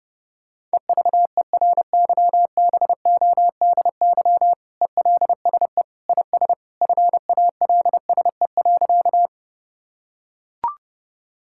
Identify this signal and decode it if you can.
Morse code, heard as a keyed tone, about 30 wpm, 700 Hz. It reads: EVERYBODY ELSE IS FALSE.
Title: Sentences from Top 1600 Words 30wpm